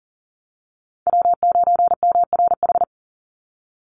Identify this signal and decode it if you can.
Morse code, heard as a keyed tone, about 40 wpm, 700 Hz. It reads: W9MRH
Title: NA SOTA Chaser Callsigns 40wpm